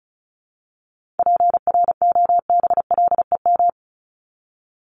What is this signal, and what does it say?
Morse code, heard as a keyed tone, about 34 wpm, 700 Hz. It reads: PROBLEM
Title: Top 500 Words Repeated 35wpm